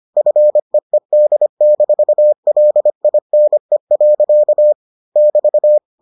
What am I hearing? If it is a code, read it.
Morse code, heard as a keyed tone, about 25 wpm, 600 Hz. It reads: FEED-LINE. <BT>